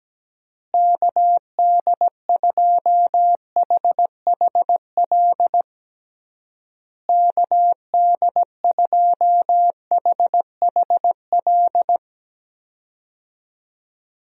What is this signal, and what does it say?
Morse code, heard as a keyed tone, about 17 wpm, 700 Hz. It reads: KD2HHL KD2HHL
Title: USA Call Signs - 6 Characters - Repeated 3x - 17wpm